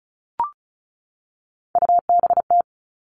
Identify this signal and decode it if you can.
Morse code, heard as a keyed tone, about 34 wpm, 700 Hz. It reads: UBT
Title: All Cut Numbers - 3-Digit 35wpm